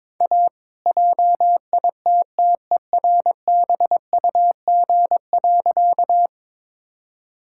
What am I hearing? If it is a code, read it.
Morse code, heard as a keyed tone, about 22 wpm, 690 Hz. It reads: A JITTERBUG.
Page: morse-code-ninja-practice-22wpm